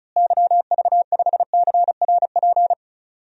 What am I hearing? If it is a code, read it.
Morse code, heard as a keyed tone, about 36 wpm, 700 Hz. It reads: YV5CRP
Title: DXCC-100 Call Signs - 6 Characters - Repeated 3x - 35wpm